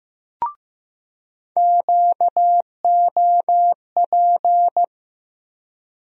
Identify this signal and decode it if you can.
Morse code, heard as a keyed tone, about 15 wpm, 700 Hz. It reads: QOP
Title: USA Call Sign Postfixes - 3 Letters 15wpm